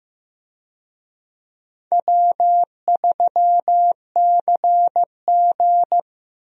Morse code, heard as a keyed tone, about 15 wpm, 700 Hz. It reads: W3CG
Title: USA Call Signs - 4 Characters - Repeated 3x - 15wpm